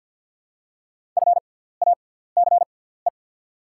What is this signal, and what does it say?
Morse code, heard as a keyed tone, about 49 wpm, spaced as for 17 wpm, 700 Hz. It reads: FACE
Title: 6x Character Spacing - Top 300 Words - 50wpm